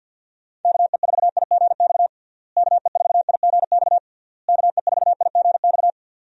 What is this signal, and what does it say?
Morse code, heard as a keyed tone, about 51 wpm, 700 Hz. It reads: KE4IGX KE4IGX KE4IGX
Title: USA Call Signs - 6 Characters - Repeated 3x - 50wpm